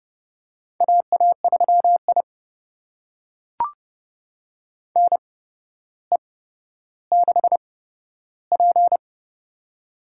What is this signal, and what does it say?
Morse code, heard as a keyed tone, about 30 wpm, spaced as for 13 wpm, 700 Hz. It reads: AA3I NE6P
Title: USA Call Signs - 4 Characters - 8x Character Spacing - 30wpm